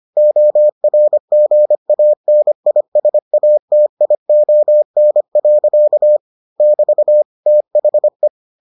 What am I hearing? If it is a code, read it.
Morse code, heard as a keyed tone, about 25 wpm, 600 Hz. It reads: ORGANISATION. <BT> THE